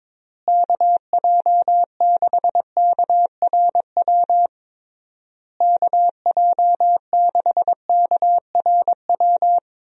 Morse code, heard as a keyed tone, about 22 wpm, 700 Hz. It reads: KJ6KRW KJ6KRW
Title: USA Call Signs - 6 Characters - Repeated 3x - 22wpm